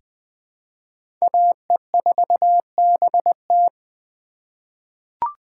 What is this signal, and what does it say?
Morse code, heard as a keyed tone, about 21 wpm, 700 Hz. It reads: AE4BT
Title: USA Call Signs - 5 Characters - 3x Character Spacing - 20wpm